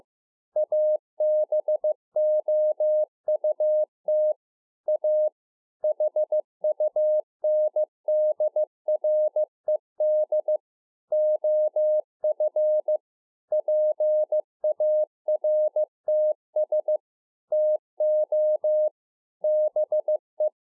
Morse code, heard as a keyed tone, about 15 wpm, 610 Hz. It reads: ABOUT A HUNDRED OF PARTS TO BE
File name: 2011-04-mp3 / A SMALL TRANCEIVER 2 15WPM